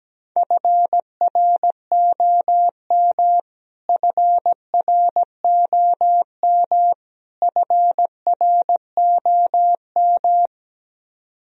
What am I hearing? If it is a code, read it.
Morse code, heard as a keyed tone, about 17 wpm, 700 Hz. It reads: FROM FROM FROM
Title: Top 100 Words Repeated 17wpm